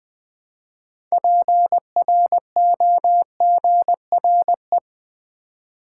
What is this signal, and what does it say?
Morse code, heard as a keyed tone, about 21 wpm, 700 Hz. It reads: PROGRE
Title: Build A Word - 3 Letters At A Time - 8-Letter Words - 20wpm